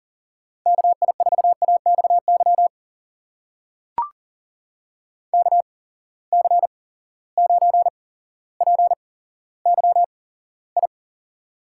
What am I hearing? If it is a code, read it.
Morse code, heard as a keyed tone, about 40 wpm, spaced as for 17 wpm, 700 Hz. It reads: KI4AXY KC9PYI
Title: USA Call Signs - 6 Characters - 8x Character Spacing - 40wpm